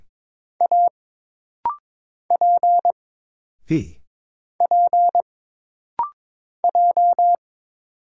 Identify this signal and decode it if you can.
Morse code, heard as a keyed tone, about 22 wpm, 700 Hz. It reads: A P P J
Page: This rapid-fire timing is geared to push your ICR (Instant Character Recognition) skills.